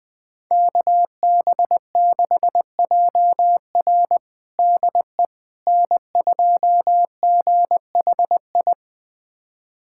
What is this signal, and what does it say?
Morse code, heard as a keyed tone, about 20 wpm, 700 Hz. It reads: KB6JR DE N2GHI
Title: USA Callsign DE Callsign 20wpm